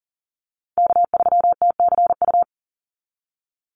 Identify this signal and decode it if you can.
Morse code, heard as a keyed tone, about 40 wpm, 700 Hz. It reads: K3TCU
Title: NA SOTA Chaser Callsigns 40wpm